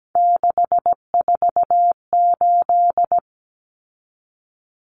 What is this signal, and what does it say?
Morse code, heard as a keyed tone, about 17 wpm, 700 Hz. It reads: 648
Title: Numbers - 3-Digit 17wpm